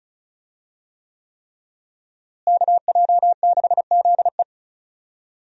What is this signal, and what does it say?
Morse code, heard as a keyed tone, about 36 wpm, 700 Hz. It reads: KJ6ZE
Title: USA Call Signs - 5 Characters - Repeated 3x - 35wpm